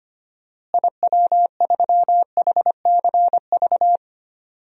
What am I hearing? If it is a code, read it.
Morse code, heard as a keyed tone, about 25 wpm, 700 Hz. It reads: IW3HCV
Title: DXCC-100 Call Signs - 6 Characters - Repeated 2x - 25wpm